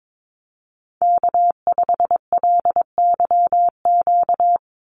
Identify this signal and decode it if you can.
Morse code, heard as a keyed tone, about 22 wpm, 700 Hz. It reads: K5LYQ
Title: USA Call Signs 22wpm